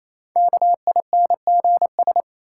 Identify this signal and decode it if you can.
Morse code, heard as a keyed tone, about 28 wpm, 700 Hz. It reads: KINGS
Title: Sentences from Top 1000 Words - Review Part 6 - New Word Drill - 28wpm